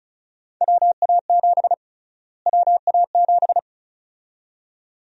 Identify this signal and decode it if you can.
Morse code, heard as a keyed tone, about 36 wpm, 700 Hz. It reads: WA7 WA7
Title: USA Call Sign Prefixes - Repeated 2x - 35wpm